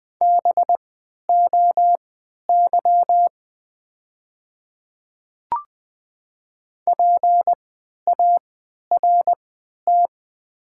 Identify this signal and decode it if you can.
Morse code, heard as a keyed tone, about 20 wpm, spaced as for 11 wpm, 700 Hz. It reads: BOY PART
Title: 3x Character Spacing - Top 500 Words - 20wpm